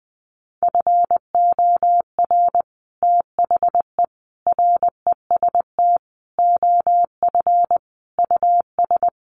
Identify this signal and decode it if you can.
Morse code, heard as a keyed tone, about 20 wpm, 700 Hz. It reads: FOR THE REST OF US
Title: Sets of 5 Words Encore 20wpm